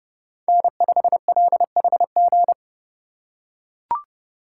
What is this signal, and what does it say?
Morse code, heard as a keyed tone, about 32 wpm, 700 Hz. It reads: N5LHG
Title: USA Call Signs - 3x Character Spacing - 30wpm